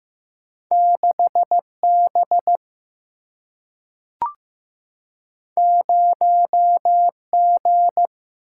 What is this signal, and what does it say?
Morse code, heard as a keyed tone, about 15 wpm, 700 Hz. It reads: 6 B 0 G
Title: Single-Letter Single-Number 15wpm